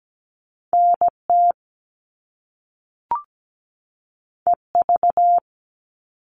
Morse code, heard as a keyed tone, about 17 wpm, 700 Hz. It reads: NT EV
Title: All Cut Numbers - 2-Digit 17wpm